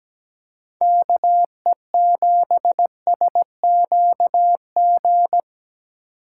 Morse code, heard as a keyed tone, about 17 wpm, 700 Hz. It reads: KE7SQG
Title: USA Call Signs - 3x Character Spacing - 17wpm